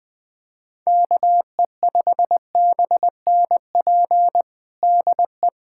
Morse code, heard as a keyed tone, about 20 wpm, 700 Hz. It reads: KE5BNP DE
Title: USA Callsign DE Callsign 20wpm